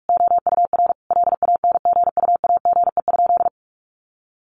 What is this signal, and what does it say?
Morse code, heard as a keyed tone, about 46 wpm, 700 Hz. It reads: OUR LANGUAGE?
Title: Sentences from Top 1100 Words 45wpm